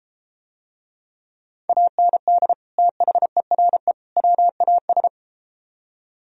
Morse code, heard as a keyed tone, about 33 wpm, 700 Hz. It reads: AND THERE WAS